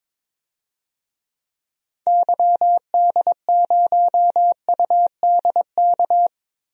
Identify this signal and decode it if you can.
Morse code, heard as a keyed tone, about 21 wpm, 700 Hz. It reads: YD0UDK
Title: DXCC-100 Call Signs - 6 Characters - Repeated 3x - 22wpm